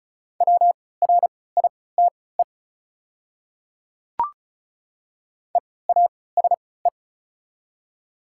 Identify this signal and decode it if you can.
Morse code, heard as a keyed tone, about 36 wpm, spaced as for 20 wpm, 700 Hz. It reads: WRITE EASE
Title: 3x Character Spacing - Top 500 Words - 35wpm